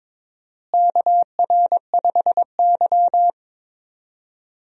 Morse code, heard as a keyed tone, about 22 wpm, 700 Hz. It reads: KR5Y
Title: USA Call Signs - 4 Characters - Repeated 2x - 22wpm